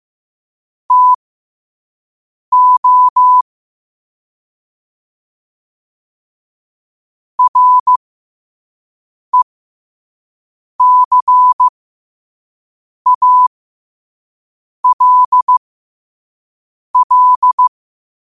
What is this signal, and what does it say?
Morse code, heard as a keyed tone, about 15 wpm, spaced as for 5 wpm, 1000 Hz. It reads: TO RECALL
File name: qotdfiles / 5wpm